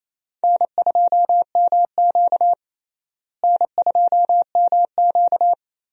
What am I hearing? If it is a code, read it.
Morse code, heard as a keyed tone, about 28 wpm, 700 Hz. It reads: N2MQ N2MQ
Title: USA Call Signs - 4 Characters - Repeated 3x - 28wpm